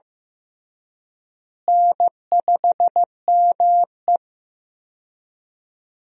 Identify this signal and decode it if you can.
Morse code, heard as a keyed tone, about 15 wpm, 700 Hz. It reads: N5ME
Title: USA Call Signs - 4 Characters - Repeated 3x - 15wpm